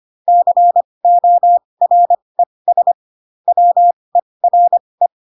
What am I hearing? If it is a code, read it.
Morse code, heard as a keyed tone, about 25 wpm, spaced as for 20 wpm, 700 Hz. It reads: CORES WERE
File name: Pinocchio / Pinocchio ch7 15 25 ik1hge